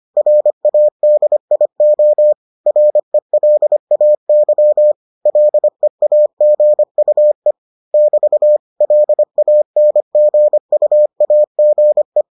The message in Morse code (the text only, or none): RADIO RELAY LEAGUE <BT> LANGUAGE